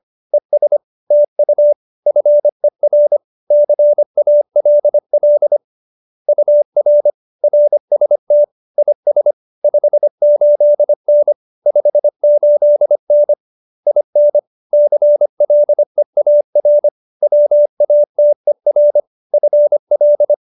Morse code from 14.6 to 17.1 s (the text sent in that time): CLEAR